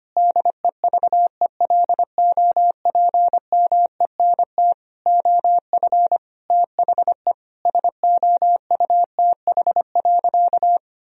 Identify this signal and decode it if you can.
Morse code, heard as a keyed tone, about 25 wpm, 700 Hz. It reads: DEVELOPMENT OF THE SOUTH.
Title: Sentences from Top 1000 Words - Review Part 4 - 25wpm